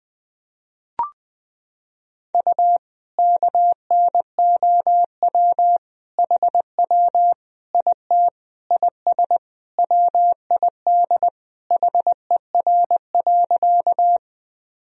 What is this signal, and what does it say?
Morse code, heard as a keyed tone, about 20 wpm, 700 Hz. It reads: U KNOW HW IT IS WID HER.
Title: Sentences from Top 100 Words and Abbreviations 20wpm